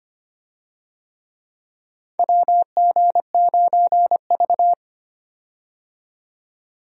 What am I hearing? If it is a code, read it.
Morse code, heard as a keyed tone, about 25 wpm, 700 Hz. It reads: WG9V
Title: USA Call Signs - 4 Characters - 25wpm